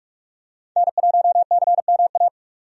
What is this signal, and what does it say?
Morse code, heard as a keyed tone, about 44 wpm, 700 Hz. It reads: N1CMA
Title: USA Call Signs - 5 Characters - 45wpm